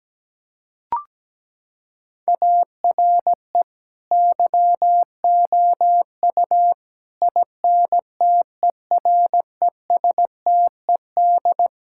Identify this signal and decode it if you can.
Morse code, heard as a keyed tone, about 17 wpm, 700 Hz. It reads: ARE YOU INTERESTED
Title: Sentences from Top 1000 Words - Review Part 6 - 17wpm